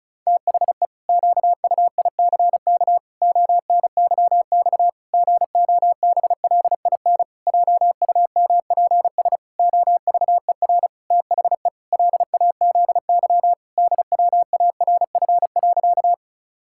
Morse code, heard as a keyed tone, about 36 wpm, 700 Hz. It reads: THE QUICK ONYX GOBLIN JUMPS OVER THE LAZY DWARF.
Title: Pangram Challenge v2 - 35wpm